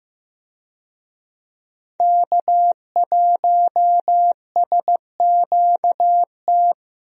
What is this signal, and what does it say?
Morse code, heard as a keyed tone, about 15 wpm, 700 Hz. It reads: K1SQT
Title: USA Call Signs - 5 Characters - Repeated 3x - 15wpm